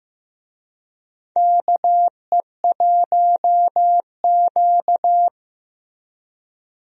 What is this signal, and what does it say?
Morse code, heard as a keyed tone, about 15 wpm, 700 Hz. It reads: KE1Q
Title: USA Call Signs - 4 Characters - Repeated 2x - 15wpm